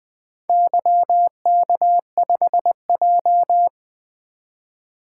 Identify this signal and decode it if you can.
Morse code, heard as a keyed tone, about 20 wpm, 700 Hz. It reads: YK5J
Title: Four Letter-Number Combinations - 20wpm